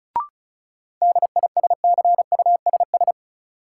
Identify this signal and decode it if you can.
Morse code, heard as a keyed tone, about 36 wpm, 700 Hz. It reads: DISCUSS
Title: Sentences from Top 1000 Words - New Word Drill - 35wpm